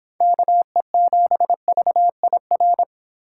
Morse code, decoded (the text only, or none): KE7VIR